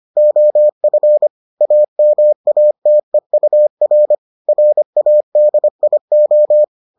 Morse code, heard as a keyed tone, about 25 wpm, 600 Hz. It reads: OF AMATEUR RADIO